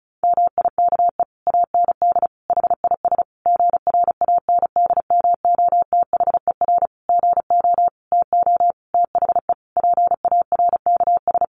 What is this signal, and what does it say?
Morse code, heard as a keyed tone, about 34 wpm, 700 Hz. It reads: MIKE AND HIS GRANDMOTHER GO TO THE PARKS
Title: Additional Easy Phrases 35wpm